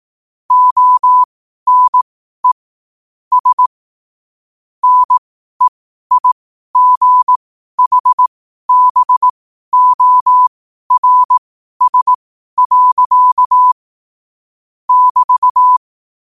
Morse code, heard as a keyed tone, about 18 wpm, spaced as for 12 wpm, 1000 Hz. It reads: ONES NEIGHBORS. <BT>
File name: qotdfiles / 12wpm